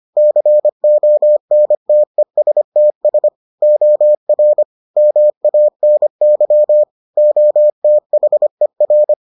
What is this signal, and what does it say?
Morse code, heard as a keyed tone, about 25 wpm, 600 Hz. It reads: CONTESTS OR MANY OTHER